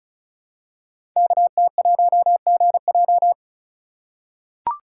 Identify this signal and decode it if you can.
Morse code, heard as a keyed tone, about 36 wpm, 690 Hz. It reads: KT1GJ
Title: Code Groups - 5 Letter:Numbers - 35wpm